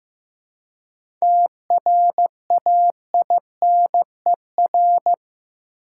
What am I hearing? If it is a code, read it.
Morse code, heard as a keyed tone, about 15 wpm, 700 Hz. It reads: TRAINER
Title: Seven Letter Words - 15wpm